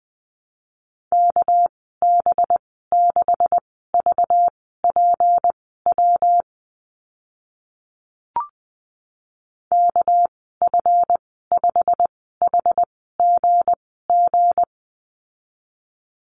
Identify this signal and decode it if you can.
Morse code, heard as a keyed tone, about 20 wpm, spaced as for 15 wpm, 700 Hz. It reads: KB6VPW KF5HGG
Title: 245 Lesson 39g - Letter z - Call Signs 20wpm